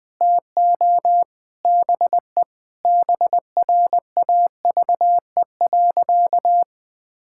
Morse code, heard as a keyed tone, about 19 wpm, 700 Hz. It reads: TO BE BRAVE.
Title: Sentences with 5-Word Phrases - 20wpm